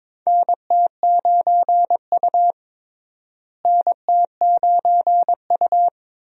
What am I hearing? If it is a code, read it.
Morse code, heard as a keyed tone, about 21 wpm, 700 Hz. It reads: NT9U NT9U
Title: USA Call Signs - 4 Characters - Repeated 3x - 22wpm